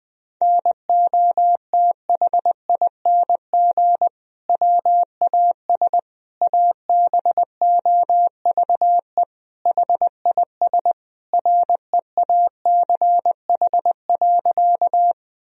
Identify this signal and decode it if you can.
Morse code, heard as a keyed tone, about 21 wpm, 700 Hz. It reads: NOTHING WAS ABOVE HIS REACH.
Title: Sentences from Top 1000 Words - Review Part 2 - 20wpm